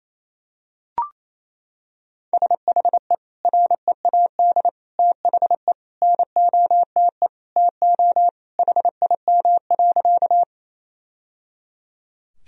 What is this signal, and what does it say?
Morse code, heard as a keyed tone, about 28 wpm, 700 Hz. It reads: SHE READ THE NOTE TO HIM.
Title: Sentences from Top 1000 Words - Review Part 2 - 28wpm